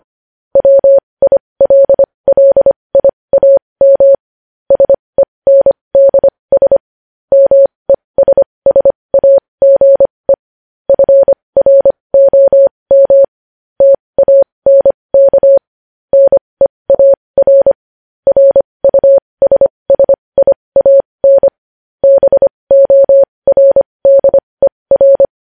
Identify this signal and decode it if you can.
Morse code, heard as a keyed tone, about 25 wpm, spaced as for 20 wpm, 550 Hz. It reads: WILLIAM SENDS MESSAGE FROM TANK NEAR RUSSIAN BORDER
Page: news-headlines-in-morse-code-at-20-wpm